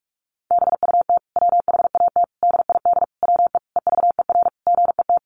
Morse code, heard as a keyed tone, about 44 wpm, 700 Hz. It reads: BUT WHAT DID WE EVER GET
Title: Sentences from Top 400 Words 45wpm